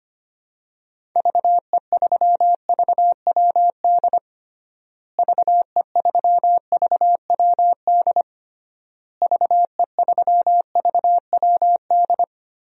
Morse code, heard as a keyed tone, about 25 wpm, 700 Hz. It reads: VE3VWD VE3VWD VE3VWD
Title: DXCC-100 Call Signs - 6 Characters - Repeated 3x - 25wpm